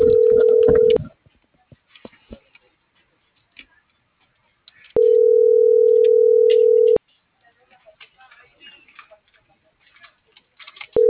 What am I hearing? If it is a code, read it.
Ringback tone (North America), 440 + 480 Hz.